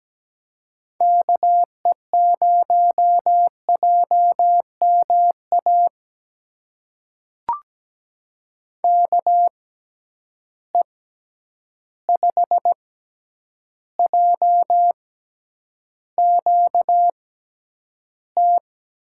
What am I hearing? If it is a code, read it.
Morse code, heard as a keyed tone, about 17 wpm, spaced as for 9 wpm, 700 Hz. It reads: KE0JMA KE5JQT
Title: USA Call Signs - 6 Characters - 6x Character Spacing - 17wpm